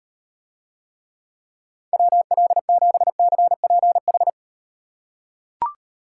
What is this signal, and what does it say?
Morse code, heard as a keyed tone, about 38 wpm, 700 Hz. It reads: WL7CPH